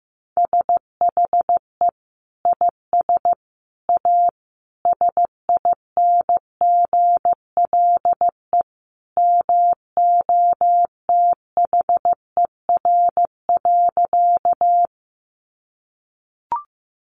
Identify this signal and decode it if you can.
Morse code, heard as a keyed tone, about 15 wpm, 700 Hz. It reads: SHE IS A SINGLE MOTHER.
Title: Sentences from Top 700 Words 15wpm